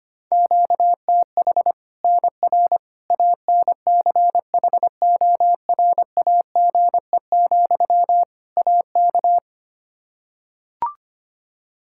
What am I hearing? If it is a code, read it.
Morse code, heard as a keyed tone, about 25 wpm, 700 Hz. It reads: QTH NR ANCHORAGE, AK
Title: K1USN SST QSO Practice 25wpm